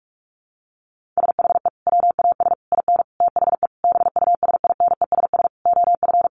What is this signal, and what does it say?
Morse code, heard as a keyed tone, about 44 wpm, 700 Hz. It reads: SHE WAS IN THE BUSINESS OF
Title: Sentences from Top 1500 Words 45wpm